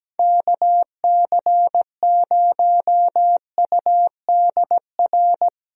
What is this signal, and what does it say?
Morse code, heard as a keyed tone, about 17 wpm, 700 Hz. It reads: KC0UDR
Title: USA Call Signs - 6 Characters - Repeated 2x - 17wpm